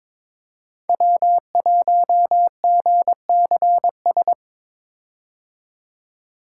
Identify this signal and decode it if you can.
Morse code, heard as a keyed tone, about 21 wpm, 700 Hz. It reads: W1GCS
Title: K1USN SST QSO Practice - 22wpm